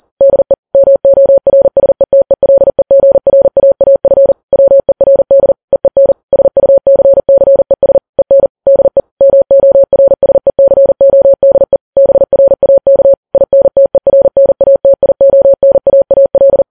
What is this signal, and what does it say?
Morse code, heard as a keyed tone, about 40 wpm, 550 Hz. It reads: DE MORSETELEGRAAF WERD EEN SUCCES EN DE MORSECODE BRAK INTERNATIONAAL
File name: Files Web tls / Morse on9fv / morse40kps